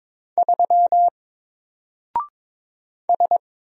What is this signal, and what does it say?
Morse code, heard as a keyed tone, about 22 wpm, 700 Hz. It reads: 3 S
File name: Single-Letter-Number-22wpm